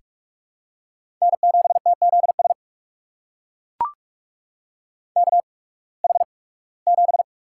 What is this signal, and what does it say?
Morse code, heard as a keyed tone, about 42 wpm, spaced as for 23 wpm, 700 Hz. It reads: N7TZS KH7
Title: USA Call Signs - 5 Characters - 8x Character Spacing - 45wpm